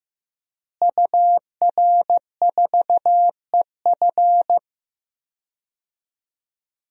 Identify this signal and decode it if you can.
Morse code, heard as a keyed tone, about 15 wpm, 700 Hz. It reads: UR4EF
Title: DXCC-100 Call Signs - 5 Characters - Repeated 2x - 15wpm